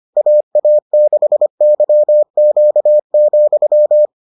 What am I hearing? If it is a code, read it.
Morse code, heard as a keyed tone, about 25 wpm, 600 Hz. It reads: AA6YQ,